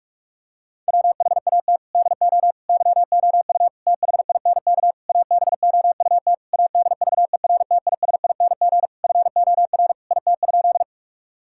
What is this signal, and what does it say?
Morse code, heard as a keyed tone, about 44 wpm, 700 Hz. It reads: WHAT DO YOU THINK ABOUT ADVERTISING FOR IT?
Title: Sentences from Top 1000 Words - Review Part 6 - 45wpm